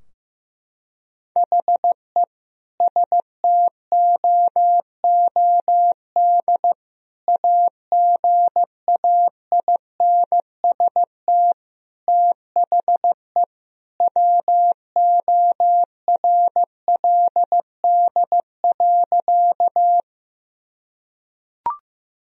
Morse code, heard as a keyed tone, about 16 wpm, 700 Hz. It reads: HE STOOD AGAINST THE WORLD.
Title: Sentences from Top 1000 Words - Review Part 2 - 15wpm